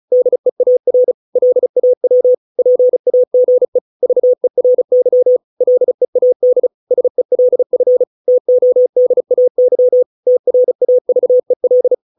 Morse code, heard as a keyed tone, about 34 wpm, 500 Hz. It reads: DEAR LAW PAGE VERY LEAD SELF TODAY TRAVEL
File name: 632 words / 632 Words 35WPM